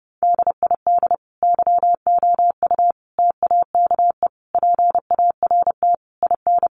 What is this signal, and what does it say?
Morse code, heard as a keyed tone, about 30 wpm, 700 Hz. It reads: DID YOU TAKE PART IN
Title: Sentences from Top 1200 Words 30wpm